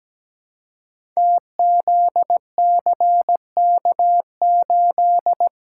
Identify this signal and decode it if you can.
Morse code, heard as a keyed tone, about 17 wpm, 700 Hz. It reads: TZCK8
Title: Code Groups - 5 Letter:Numbers - 17wpm